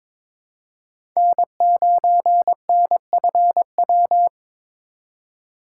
Morse code, heard as a keyed tone, about 22 wpm, 700 Hz. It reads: N9NFW
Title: USA Call Signs - 2x Character Spacing - 22wpm